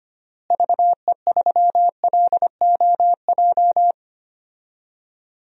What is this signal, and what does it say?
Morse code, heard as a keyed tone, about 25 wpm, 700 Hz. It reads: VE3LOJ
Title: DXCC-100 Call Signs - 6 Characters - Repeated 3x - 25wpm